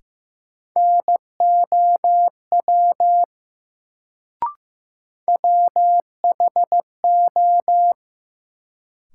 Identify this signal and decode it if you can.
Morse code, heard as a keyed tone, about 15 wpm, 700 Hz. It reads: NOW WHO
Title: Sentences from Top 100 Words - New Word Drill - 15wpm